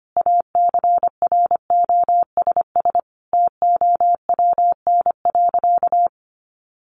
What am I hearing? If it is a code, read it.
Morse code, heard as a keyed tone, about 25 wpm, 700 Hz. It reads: ACROSS TOWN.
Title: Sentences from Top 1500 Words 25wpm